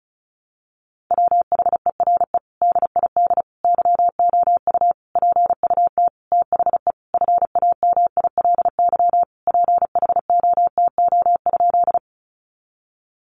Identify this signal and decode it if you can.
Morse code, heard as a keyed tone, about 34 wpm, 700 Hz. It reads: WHERE DID YOU PUT THE FAMILY PHOTO?
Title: Sentences from Top 1200 Words 35wpm